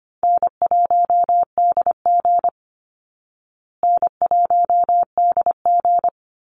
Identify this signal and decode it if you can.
Morse code, heard as a keyed tone, about 25 wpm, 700 Hz. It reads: N1DG N1DG
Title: Call Signs Repeated 3x - 25wpm